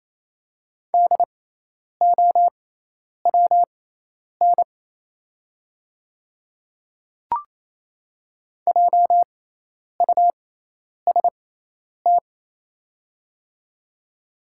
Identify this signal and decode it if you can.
Morse code, heard as a keyed tone, about 28 wpm, spaced as for 10 wpm, 700 Hz. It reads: DOWN JUST
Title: 6x Character Spacing - Top 200 Words - 28wpm